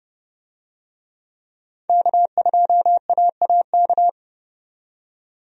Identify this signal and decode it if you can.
Morse code, heard as a keyed tone, about 29 wpm, 700 Hz. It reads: K2AAK
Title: USA Call Signs - 5 Characters - 30wpm